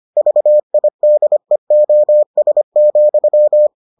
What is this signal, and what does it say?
Morse code, heard as a keyed tone, about 25 wpm, 600 Hz. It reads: VIDEOS,